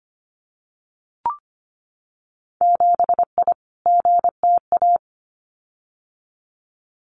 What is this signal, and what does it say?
Morse code, heard as a keyed tone, about 25 wpm, 690 Hz. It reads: 7I GTA
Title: Winter Field Day Exchanges Encore 25wpm